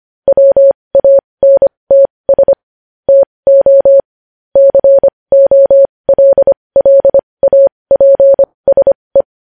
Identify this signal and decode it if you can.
Morse code, heard as a keyed tone, about 25 wpm, spaced as for 20 wpm, 550 Hz. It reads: WANTS TO COLLAPSE
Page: News Headlines in Morse Code at 20 WPM